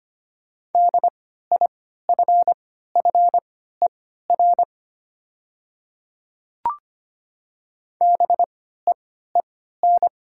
Morse code, heard as a keyed tone, about 25 wpm, spaced as for 14 wpm, 700 Hz. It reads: DIFFER BEEN
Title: 3x Character Spacing - Top 200 Words - 25wpm